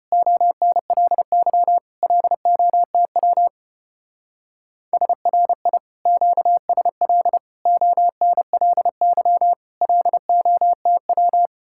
Morse code, heard as a keyed tone, about 32 wpm, 700 Hz. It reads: ONLY LOTW SRI QSL ONLY LOTW